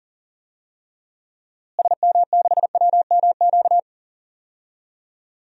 Morse code, heard as a keyed tone, about 37 wpm, 700 Hz. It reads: SM6WMQ
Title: DXCC-100 Call Signs - 6 Characters - Repeated 3x - 40wpm